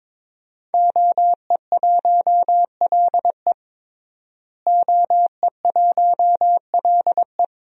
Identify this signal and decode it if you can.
Morse code, heard as a keyed tone, about 21 wpm, 700 Hz. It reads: OE1LE OE1LE
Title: DXCC-100 Call Signs - 5 Characters - Repeated 3x - 22wpm